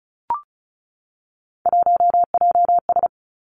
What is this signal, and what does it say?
Morse code, heard as a keyed tone, about 36 wpm, 700 Hz. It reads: 1JS
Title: Three Challenging Letter-Number Combinations 35wpm